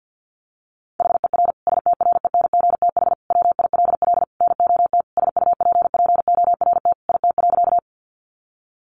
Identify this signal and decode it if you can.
Morse code, heard as a keyed tone, about 50 wpm, 700 Hz. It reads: HER STRENGTH WILL NOT SUPPORT IT.
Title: Sentences from Top 1300 Words 50wpm